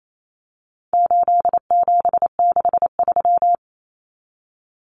Morse code, heard as a keyed tone, about 28 wpm, 700 Hz. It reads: 8763
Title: Common Cut Numbers - 4-Digit 28wpm